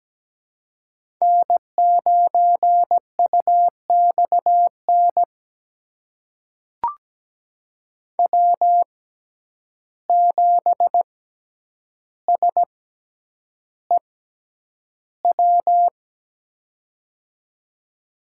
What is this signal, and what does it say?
Morse code, heard as a keyed tone, about 17 wpm, spaced as for 9 wpm, 700 Hz. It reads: N9UXN W7SEW
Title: USA Call Signs - 5 Characters - 6x Character Spacing - 17wpm